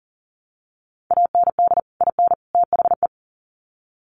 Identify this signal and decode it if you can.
Morse code, heard as a keyed tone, about 40 wpm, 700 Hz. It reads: AND IN THE
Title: Sets of 3 Words 40wpm